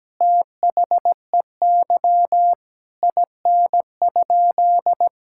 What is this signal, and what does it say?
Morse code, heard as a keyed tone, about 17 wpm, 700 Hz. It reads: THEY IN?
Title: Sentences from Top 1000 Words - Review Part 5 - 17wpm